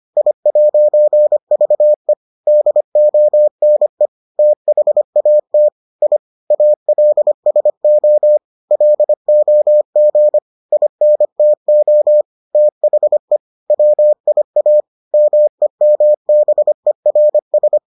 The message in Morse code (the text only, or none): I'VE DONE THAT I ALSO LOG INTO THE WIA MEMBERS